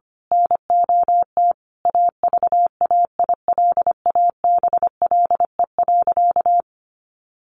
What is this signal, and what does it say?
Morse code, heard as a keyed tone, about 25 wpm, 700 Hz. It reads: NOT AVAILABLE.
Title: Sentences from Top 600 Words 25wpm